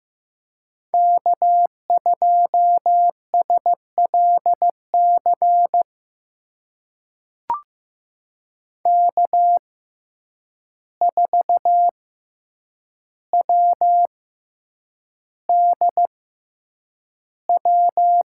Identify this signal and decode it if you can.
Morse code, heard as a keyed tone, about 15 wpm, spaced as for 8 wpm, 700 Hz. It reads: K2SLC K4WDW
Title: USA Call Signs - 5 Characters - 6x Character Spacing - 15wpm